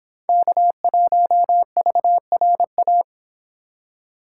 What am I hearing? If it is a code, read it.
Morse code, heard as a keyed tone, about 29 wpm, 700 Hz. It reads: K1VRA